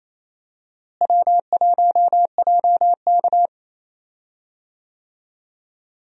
Morse code, heard as a keyed tone, about 28 wpm, 700 Hz. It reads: W1JK
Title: USA Call Signs - 4 Characters - Repeated 3x - 28wpm